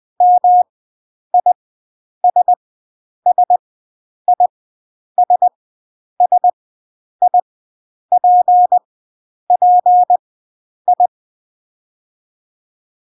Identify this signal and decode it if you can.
Morse code, heard as a keyed tone, about 20 wpm, spaced as for 9 wpm, 720 Hz. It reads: MISSISSIPPI